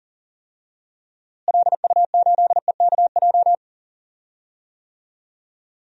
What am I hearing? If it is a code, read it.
Morse code, heard as a keyed tone, about 40 wpm, 700 Hz. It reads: LU8EKJ
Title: DXCC-100 Call Signs - 6 Characters - Repeated 3x - 40wpm